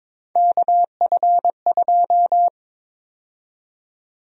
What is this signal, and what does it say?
Morse code, heard as a keyed tone, about 22 wpm, 700 Hz. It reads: KF2
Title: USA Call Sign Prefixes - Repeated 2x - 22wpm